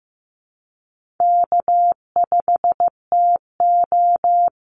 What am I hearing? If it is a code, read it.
Morse code, heard as a keyed tone, about 15 wpm, 700 Hz. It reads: K5TO
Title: Easy Call Signs - 15wpm